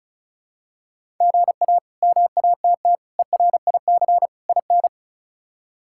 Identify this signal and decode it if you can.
Morse code, heard as a keyed tone, about 34 wpm, 700 Hz. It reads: GA MATT ERIC IN
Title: K1USN SST QSO Practice - 35wpm